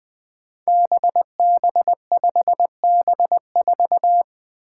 Morse code, heard as a keyed tone, about 21 wpm, 700 Hz. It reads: BB5B4
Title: Five Challenging Letter-Number Combinations 20wpm